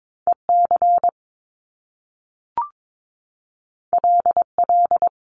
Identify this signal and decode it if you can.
Morse code, heard as a keyed tone, about 22 wpm, 700 Hz. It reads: EC LL
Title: Top Two Letter Combinations in Words 22wpm